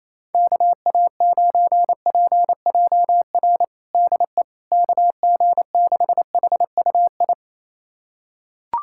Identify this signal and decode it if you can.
Morse code, heard as a keyed tone, about 28 wpm, 700 Hz. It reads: KA9PJR DE KG6HUI
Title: USA Callsign DE Callsign 28wpm